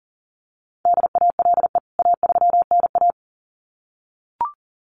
Morse code, heard as a keyed tone, about 40 wpm, 700 Hz. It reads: DALE A3NA
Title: CWT Exchange 40wpm